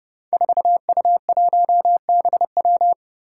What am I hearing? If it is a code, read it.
Morse code, heard as a keyed tone, about 30 wpm, 700 Hz. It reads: 4U1BW
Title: Five Challenging Letter-Number Combinations 30wpm